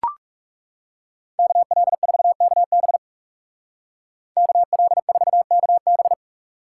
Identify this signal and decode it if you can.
Morse code, heard as a keyed tone, about 37 wpm, 700 Hz. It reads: KL4KB KL4KB
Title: Speed Racing - USA 2x2 Call Signs 30wpm